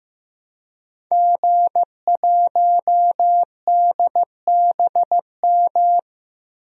Morse code, heard as a keyed tone, about 15 wpm, 700 Hz. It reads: G1DBM
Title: DXCC-100 Call Signs - 5 Characters - Repeated 3x - 15wpm